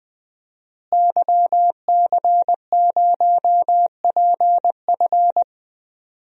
Morse code, heard as a keyed tone, about 20 wpm, 700 Hz. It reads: YC0PF
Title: DXCC-100 Call Signs - 5 Characters - Repeated 2x - 20wpm